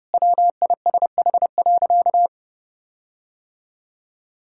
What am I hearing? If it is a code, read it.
Morse code, heard as a keyed tone, about 30 wpm, 700 Hz. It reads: WISH.
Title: Sentences from Top 1600 Words 30wpm